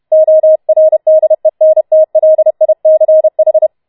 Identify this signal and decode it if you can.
Morse code, heard as a keyed tone, about 31 wpm, 630 Hz. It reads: ORDENTLICH
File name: Schneewittchen Teil4 32wpm